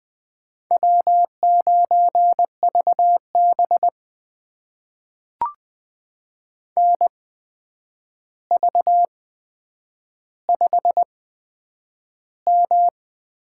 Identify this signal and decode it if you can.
Morse code, heard as a keyed tone, about 20 wpm, spaced as for 9 wpm, 700 Hz. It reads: W9VB NV5M
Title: USA Call Signs - 4 Characters - 8x Character Spacing - 20wpm